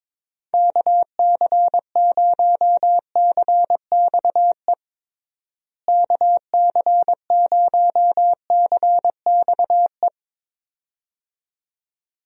Morse code, heard as a keyed tone, about 22 wpm, 700 Hz. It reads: KC0CXE KC0CXE
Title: USA Call Signs - 6 Characters - Repeated 2x - 22wpm